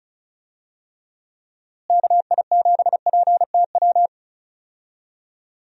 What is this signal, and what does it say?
Morse code, heard as a keyed tone, about 34 wpm, 700 Hz. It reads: KI7PTW
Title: USA Call Signs - 6 Characters - Repeated 2x - 35wpm